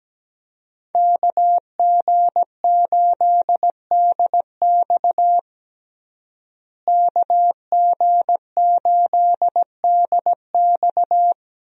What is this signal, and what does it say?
Morse code, heard as a keyed tone, about 17 wpm, 700 Hz. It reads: KG8DX KG8DX
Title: USA Call Signs - 5 Characters - Repeated 2x - 17wpm